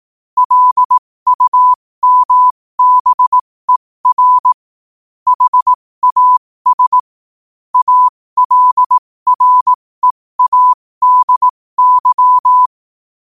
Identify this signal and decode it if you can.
Morse code, heard as a keyed tone, about 18 wpm, spaced as for 15 wpm, 1000 Hz. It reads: LUMBER HAS ALREADY
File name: qotdfiles / 15wpm